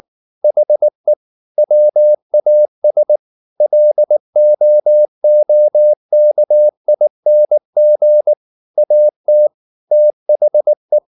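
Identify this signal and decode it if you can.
Morse code, heard as a keyed tone, about 19 wpm, 600 Hz. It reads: HE WAS LOOKING AT THE
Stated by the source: Morse code at 19 wpm from CW med Gnister, reading Ken Follett's novel Never in English.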